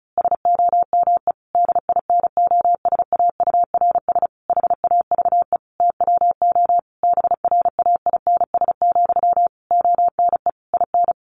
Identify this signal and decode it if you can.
Morse code, heard as a keyed tone, about 34 wpm, 700 Hz. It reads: SOME DINOSAURS HAVE TWO BRAINS, ONE IN
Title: A Sound Of Thunder 35wpm